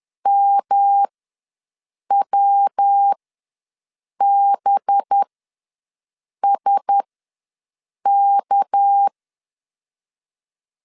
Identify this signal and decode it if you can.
Morse code, heard as a keyed tone, about 10 wpm, spaced as for 6 wpm, 800 Hz. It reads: MWBSK
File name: LEZIONI / 7 GF9Z / 7 GF9 ws 10wpm 4034